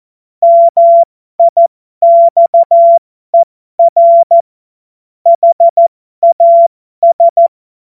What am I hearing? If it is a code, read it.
Morse code, heard as a keyed tone, about 14 wpm, spaced as for 12 wpm, 680 Hz. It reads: MIXER HAS
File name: Morsekurs / Witze